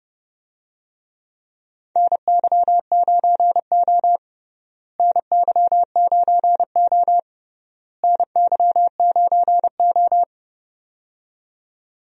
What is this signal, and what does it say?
Morse code, heard as a keyed tone, about 30 wpm, 700 Hz. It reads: NY9O NY9O NY9O
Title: USA Call Signs - 4 Characters - Repeated 3x - 30wpm